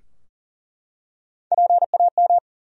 Morse code, heard as a keyed tone, about 40 wpm, 700 Hz. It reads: PAM
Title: USA Common Names - Max 4 Letters - 40wpm